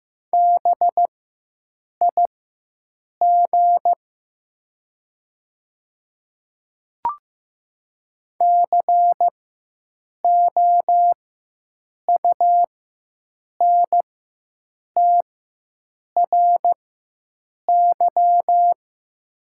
Morse code, heard as a keyed tone, about 15 wpm, spaced as for 7 wpm, 700 Hz. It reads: BIG COUNTRY
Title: 4x Character Spacing - Top 500 Words - 15wpm